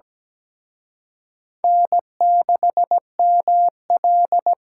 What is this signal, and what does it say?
Morse code, heard as a keyed tone, about 20 wpm, 700 Hz. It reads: N6ML
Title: Speed Racing - USA 1x2 Call Signs 17wpm